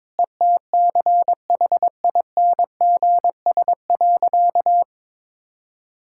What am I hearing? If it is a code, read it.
Morse code, heard as a keyed tone, about 21 wpm, 700 Hz. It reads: ETCHINGS.
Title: Pangram Challenge v2 - 22wpm